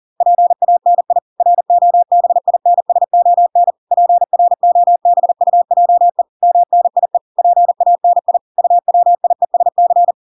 Morse code, heard as a keyed tone, about 40 wpm, 700 Hz. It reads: PANI ROBINSON PROBUJE MNIE PANI UWIESC